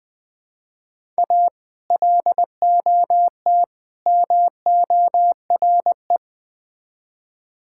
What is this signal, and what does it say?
Morse code, heard as a keyed tone, about 20 wpm, 700 Hz. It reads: A LOT MORE
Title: Build A Phrase - 3-Word Phrases - 20wpm